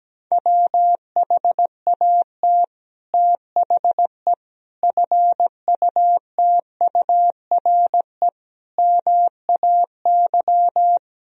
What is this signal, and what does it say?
Morse code, heard as a keyed tone, about 17 wpm, 700 Hz. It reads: WHAT THE FUTURE MAY
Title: Sentences from Top 1000 Words - Review Part 2 - 17wpm